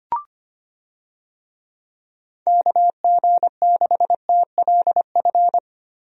Morse code, heard as a keyed tone, about 25 wpm, 700 Hz. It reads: KG6TLF
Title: USA Call Signs - 6 Characters - Repeated 3x - 25wpm